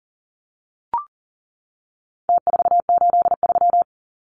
Morse code, heard as a keyed tone, about 40 wpm, 700 Hz. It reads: T483
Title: Numbers - Contest Style - 4-Digit 40wpm